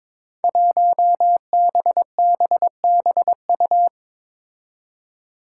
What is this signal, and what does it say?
Morse code, heard as a keyed tone, about 22 wpm, 700 Hz. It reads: 1BBBU
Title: Five Challenging Letter-Number Combinations 22wpm